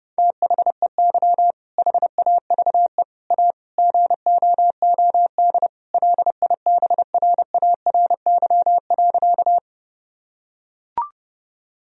Morse code, heard as a keyed tone, about 32 wpm, 700 Hz. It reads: THEY HAVE A GOOD LIBRARY.
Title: Sentences from Top 1000 Words - Review Part 4 - 30wpm